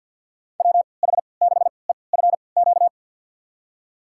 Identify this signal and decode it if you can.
Morse code, heard as a keyed tone, about 46 wpm, spaced as for 27 wpm, 700 Hz. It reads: WH6EFQ
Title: USA Call Signs - 3x Character Spacing - 50wpm